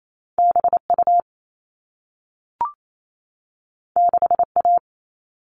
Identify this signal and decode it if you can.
Morse code, heard as a keyed tone, about 28 wpm, 700 Hz. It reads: BU 6A
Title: All Cut Numbers - 2-Digit 28wpm